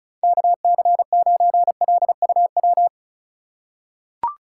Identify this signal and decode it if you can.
Morse code, heard as a keyed tone, about 36 wpm, 700 Hz. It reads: KC9LUW
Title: USA Call Signs - 6 Characters - 2x Character Spacing - 35wpm